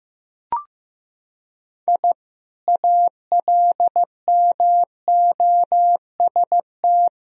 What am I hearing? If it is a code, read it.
Morse code, heard as a keyed tone, about 15 wpm, 700 Hz. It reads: I ALMOST
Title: Sentences from Top 1000 Words - Review Part 4 - 15wpm